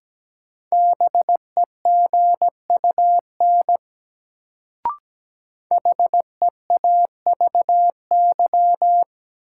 Morse code, heard as a keyed tone, about 17 wpm, 700 Hz. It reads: BEGUN HEAVY
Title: Sentences from Top 1000 Words - Review Part 6 - New Word Drill - 17wpm